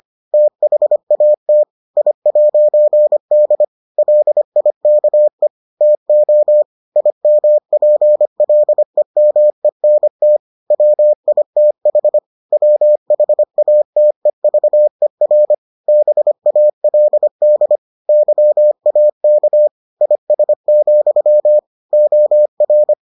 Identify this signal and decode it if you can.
Morse code, heard as a keyed tone, about 25 wpm, 600 Hz. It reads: THAT I'D LIKE TO IMPLEMENT WITH WHATEVER BALD YAK IS, OR